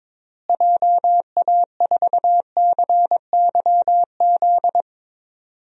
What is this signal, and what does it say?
Morse code, heard as a keyed tone, about 21 wpm, 700 Hz. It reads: JA4CYZ
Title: DXCC-100 Call Signs - 6 Characters - Repeated 3x - 22wpm